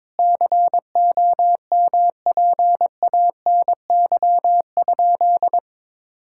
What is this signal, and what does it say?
Morse code, heard as a keyed tone, about 22 wpm, 700 Hz. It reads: COMPANY?
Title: Sentences from Top 1600 Words 22wpm